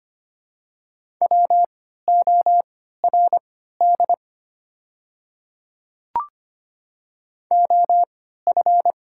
Morse code, heard as a keyed tone, about 25 wpm, spaced as for 14 wpm, 690 Hz. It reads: WORD OF
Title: 3x Character Spacing - Top 100 Words - 25wpm